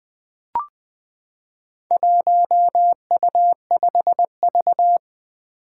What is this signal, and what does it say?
Morse code, heard as a keyed tone, about 20 wpm, 700 Hz. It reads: 1U5V
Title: Four Challenging Letter-Number Combinations 20wpm